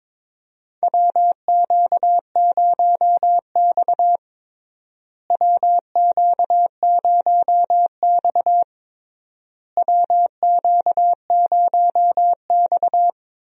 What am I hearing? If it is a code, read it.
Morse code, heard as a keyed tone, about 22 wpm, 700 Hz. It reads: WQ0X WQ0X WQ0X
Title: USA Call Signs - 4 Characters - Repeated 3x - 22wpm